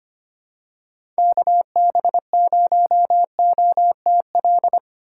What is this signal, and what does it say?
Morse code, heard as a keyed tone, about 25 wpm, 700 Hz. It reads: KB0OTL
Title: USA Call Signs - 6 Characters - 4x Character Spacing - 25wpm